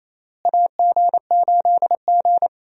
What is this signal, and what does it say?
Morse code, heard as a keyed tone, about 28 wpm, 700 Hz. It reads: AG8G
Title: USA Call Signs - 4 Characters - Repeated 3x - 28wpm